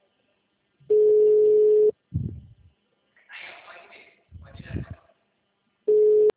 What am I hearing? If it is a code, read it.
Ringback tone (Europe, 425 Hz), with speech.